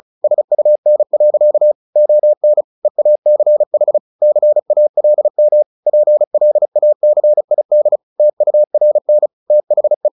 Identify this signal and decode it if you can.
Morse code, heard as a keyed tone, about 34 wpm, 600 Hz. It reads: SUN. ON EACH CALM PLACID TURN THE
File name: the old man and the sea 2 35wpm